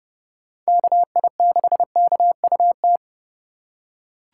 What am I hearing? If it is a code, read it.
Morse code, heard as a keyed tone, about 30 wpm, 700 Hz. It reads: KI6KUT
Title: USA Call Signs - 8x Character Spacing - 30wpm